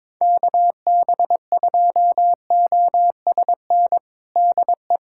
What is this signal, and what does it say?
Morse code, heard as a keyed tone, about 22 wpm, 700 Hz. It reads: KB2OSN DE
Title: USA Callsign DE Callsign 22wpm